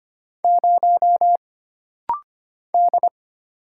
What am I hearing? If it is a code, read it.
Morse code, heard as a keyed tone, about 25 wpm, 700 Hz. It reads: 0 D
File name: Single Character (Letter:Number:Punctuation:Prosign) - Rapid-Fire - 25wpm